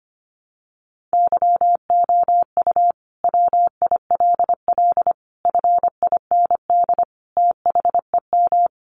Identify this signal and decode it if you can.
Morse code, heard as a keyed tone, about 25 wpm, 700 Hz. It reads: YOU WILL FIND THEM
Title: Sentences from Top 700 Words 25wpm